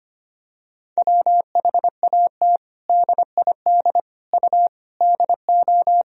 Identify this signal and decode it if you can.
Morse code, heard as a keyed tone, about 25 wpm, 700 Hz. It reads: WHAT DID U DO
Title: Sentences from Top 100 Words and Abbreviations 25wpm